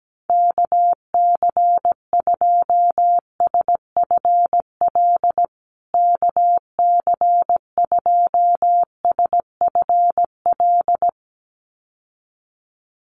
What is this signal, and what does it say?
Morse code, heard as a keyed tone, about 17 wpm, 700 Hz. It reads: KC2SFL KC2SFL
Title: USA Call Signs Repeated 2x 17wpm